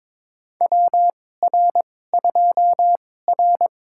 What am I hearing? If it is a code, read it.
Morse code, heard as a keyed tone, about 23 wpm, spaced as for 16 wpm, 700 Hz. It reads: WR2R
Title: USA Call Signs - 4 Characters - 2x Character Spacing - 22wpm